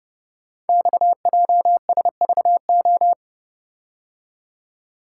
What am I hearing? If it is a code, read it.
Morse code, heard as a keyed tone, about 30 wpm, 700 Hz. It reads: XJSVO
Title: Code Groups - 5 Letters - 30wpm